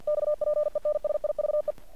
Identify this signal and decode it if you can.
Morse code, heard as a keyed tone, about 50 wpm, 600 Hz. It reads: XPENSIVE
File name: archived / CRSnet